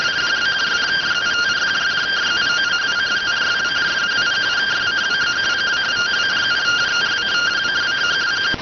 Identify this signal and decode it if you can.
Two-tone teleprinter signal, about 170 Hz shift.